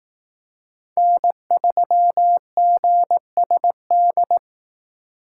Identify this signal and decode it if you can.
Morse code, heard as a keyed tone, about 18 wpm, 700 Hz. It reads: N3GSD
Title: Speed Racing - USA 1x3 Call Signs 15wpm